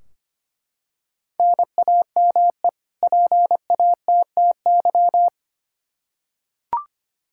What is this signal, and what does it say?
Morse code, heard as a keyed tone, about 25 wpm, 700 Hz. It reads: NAME PATTY
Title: K1USN SST QSO Practice 25wpm